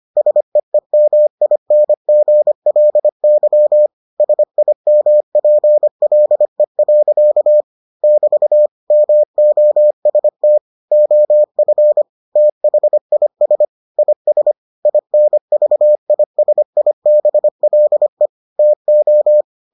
Morse code, heard as a keyed tone, about 25 wpm, 600 Hz. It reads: SEEMINGLY SIMPLE. <BT> MOST OF THIS IS INVISIBLE TO